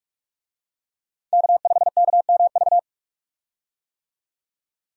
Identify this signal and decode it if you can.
Morse code, heard as a keyed tone, about 46 wpm, 700 Hz. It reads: K5KMV
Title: USA Call Signs - 5 Characters - 45wpm